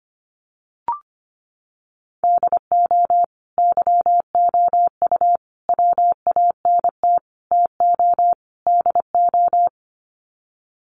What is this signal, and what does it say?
Morse code, heard as a keyed tone, about 25 wpm, 700 Hz. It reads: DO YOU WANT TO DO
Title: Sets of 5 Words Encore 25wpm